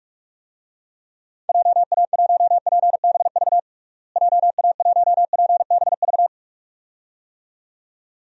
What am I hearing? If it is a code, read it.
Morse code, heard as a keyed tone, about 41 wpm, 700 Hz. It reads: JA1PBV JA1PBV
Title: DXCC-100 Call Signs - 6 Characters - Repeated 2x - 45wpm